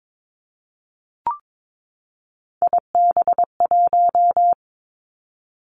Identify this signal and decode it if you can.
Morse code, heard as a keyed tone, about 22 wpm, 700 Hz. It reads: IB1
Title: Three Challenging Letter-Number Combinations 22wpm